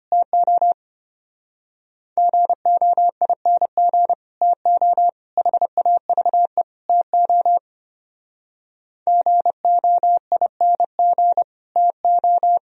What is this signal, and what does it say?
Morse code, heard as a keyed tone, about 30 wpm, 700 Hz. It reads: TO GOING TO HAVE TO GOING TO
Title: Speed-Racing - Sets of 4 Words 25wpm